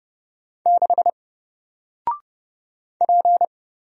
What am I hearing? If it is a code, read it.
Morse code, heard as a keyed tone, about 30 wpm, 700 Hz. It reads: 6 P
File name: Single-Letter-Number-30wpm